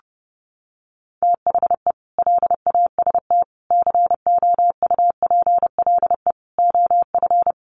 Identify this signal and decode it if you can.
Morse code, heard as a keyed tone, about 30 wpm, 700 Hz. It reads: THE LAST COUPLE OF
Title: Sets of 5 Words Encore2 30wpm